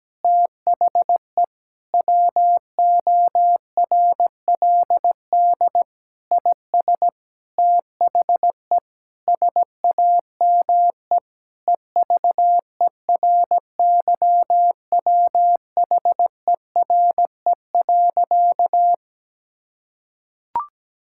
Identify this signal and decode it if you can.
Morse code, heard as a keyed tone, about 17 wpm, 700 Hz. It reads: THE WORLD IS THE SAME EVERYWHERE.
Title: Sentences from Top 1600 Words 17wpm